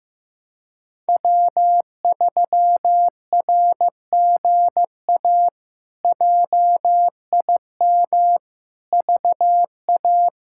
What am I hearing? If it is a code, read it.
Morse code, heard as a keyed tone, about 16 wpm, 700 Hz. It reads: W3RGA JIM VA
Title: K1USN SST QSO Practice - 15wpm